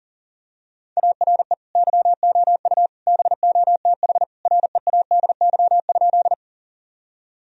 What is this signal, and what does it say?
Morse code, heard as a keyed tone, about 40 wpm, 700 Hz. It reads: ARE YOU BOTH READY?
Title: Sentences from Top 1000 Words - Review Part 1 - 40wpm